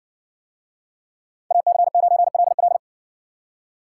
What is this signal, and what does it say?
Morse code, heard as a keyed tone, about 60 wpm, 700 Hz. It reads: AC9LL